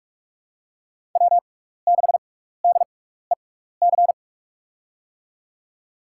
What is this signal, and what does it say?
Morse code, heard as a keyed tone, about 47 wpm, spaced as for 16 wpm, 700 Hz. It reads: W6DEC
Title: USA Call Signs - 6x Character Spacing - 45wpm